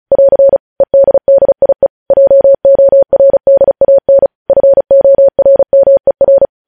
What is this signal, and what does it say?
Morse code, heard as a keyed tone, about 36 wpm, 550 Hz. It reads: <AR> EDDIE JORDAN FORMER
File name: podcast / News headlines-1x-35x35